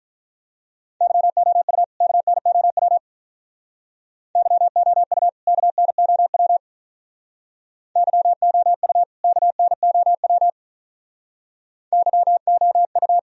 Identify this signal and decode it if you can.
Morse code, heard as a keyed tone, about 45 wpm, 700 Hz. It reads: YOU KNOW YOU KNOW YOU KNOW YOU